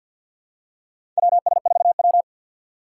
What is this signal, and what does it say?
Morse code, heard as a keyed tone, about 47 wpm, 700 Hz. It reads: WS4W
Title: USA Call Signs - 4 Characters - 6x Character Spacing - 50wpm